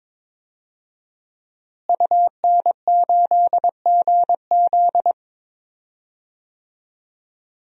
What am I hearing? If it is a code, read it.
Morse code, heard as a keyed tone, about 22 wpm, 700 Hz. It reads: UN8GZ
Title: DXCC-100 Call Signs - 5 Characters - Repeated 3x - 22wpm